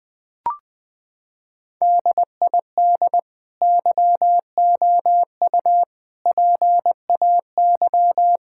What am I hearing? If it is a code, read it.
Morse code, heard as a keyed tone, about 21 wpm, 700 Hz. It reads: DID YOU PAY
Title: Sentences from Top 1000 Words - Review Part 5 - 20wpm